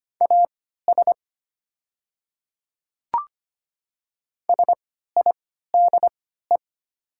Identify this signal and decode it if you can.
Morse code, heard as a keyed tone, about 25 wpm, spaced as for 14 wpm, 700 Hz. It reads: AS SIDE
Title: 3x Character Spacing - Top 100 Words - 25wpm